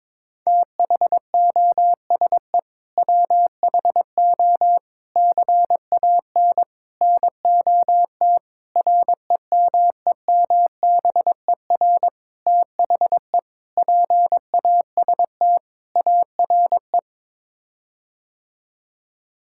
Morse code, heard as a keyed tone, about 22 wpm, 700 Hz. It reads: THOSE WHO CAN NOT REMEMBER THE PAST ARE
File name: practice / 22wpm / Sets of 8 Words - 22wpm